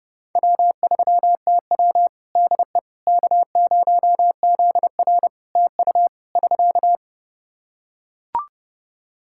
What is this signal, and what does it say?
Morse code, heard as a keyed tone, about 30 wpm, 700 Hz. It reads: W3TW DE K0ZR TU <SK>
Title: K1USN SST QSO Practice 30wpm